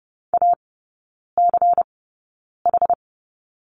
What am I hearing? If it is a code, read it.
Morse code, heard as a keyed tone, about 30 wpm, spaced as for 9 wpm, 700 Hz. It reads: ACH
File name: Lesson 16e - Letter c - Three Characters 30wpm